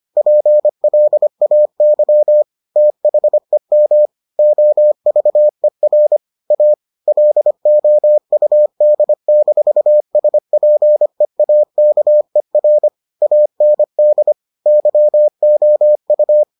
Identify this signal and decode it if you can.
Morse code, heard as a keyed tone, about 25 wpm, 600 Hz. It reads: PLAY THEM OVER A LOUD-SPEAKER AND YOU